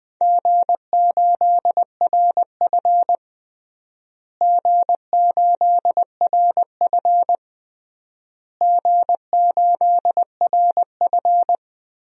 Morse code, heard as a keyed tone, about 21 wpm, 700 Hz. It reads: G8RF G8RF G8RF
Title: DXCC-100 Call Signs - 4 Characters - Repeated 3x - 20wpm